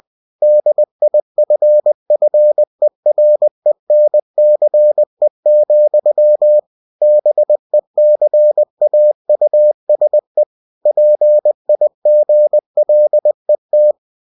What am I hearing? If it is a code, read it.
Morse code, heard as a keyed tone, about 20 wpm, 600 Hz. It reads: DIFFERENCE, BECAUSE PIGLET
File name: cw-books / winnie-the-pooh / winnie-20wpm-page-0063